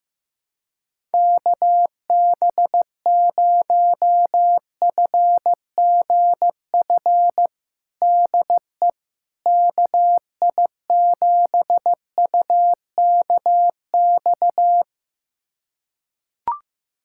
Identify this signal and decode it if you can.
Morse code, heard as a keyed tone, about 15 wpm, 700 Hz. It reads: KB0FGF DE KI7UKX
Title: USA Callsign DE Callsign - 6 Characters - 15wpm